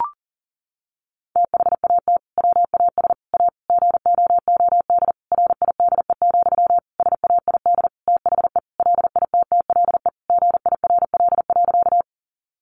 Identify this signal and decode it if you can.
Morse code, heard as a keyed tone, about 37 wpm, 700 Hz. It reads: THAT WAS A GOOD RIDE, SAID THE LITTLE GIRL.
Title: Sentences from Top 1200 Words 40wpm